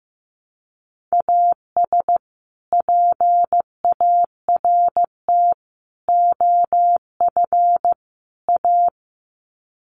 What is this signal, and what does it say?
Morse code, heard as a keyed tone, about 15 wpm, 700 Hz. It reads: AS PART OF A
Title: Sets of 4 Words 15wpm